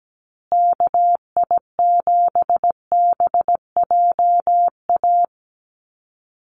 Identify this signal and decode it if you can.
Morse code, heard as a keyed tone, about 17 wpm, 700 Hz. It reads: KI7BJA
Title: USA Call Signs Repeated 3x 17wpm